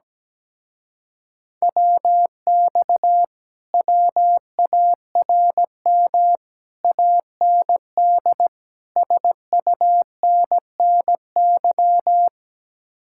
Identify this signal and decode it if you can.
Morse code, heard as a keyed tone, about 17 wpm, 700 Hz. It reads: WX WARM AND SUNNY
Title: Speed-Racing Common QSO Phrases - 17wpm